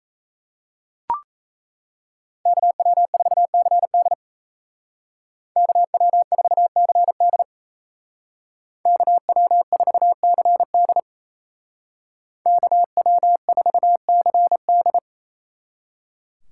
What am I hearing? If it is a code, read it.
Morse code, heard as a keyed tone, about 36 wpm, 700 Hz. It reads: KW4CD KW4CD KW4CD KW4CD